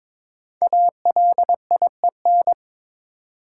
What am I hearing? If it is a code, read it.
Morse code, heard as a keyed tone, about 22 wpm, 700 Hz. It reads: ALIEN
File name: 5 Letter Words - First 3 Letters Given - 22wpm